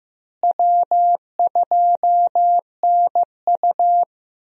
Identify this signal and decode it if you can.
Morse code, heard as a keyed tone, about 16 wpm, 700 Hz. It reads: W2NU
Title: USA Call Signs - 4 Characters - Repeated 3x - 15wpm